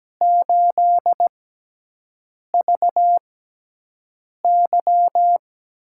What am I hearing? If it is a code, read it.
Morse code, heard as a keyed tone, about 17 wpm, spaced as for 6 wpm, 700 Hz. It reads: 8VY
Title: USA Call Signs - 6x Character Spacing - 17wpm